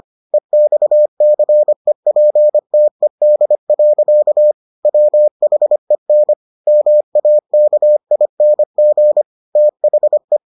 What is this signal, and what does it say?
Morse code, heard as a keyed tone, about 25 wpm, 600 Hz. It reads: EXCEPTED. WHEN MAKING THE